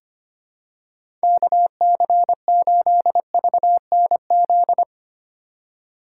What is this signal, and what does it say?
Morse code, heard as a keyed tone, about 25 wpm, 700 Hz. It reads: KC8VNZ
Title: USA Call Signs - 3x Character Spacing - 25wpm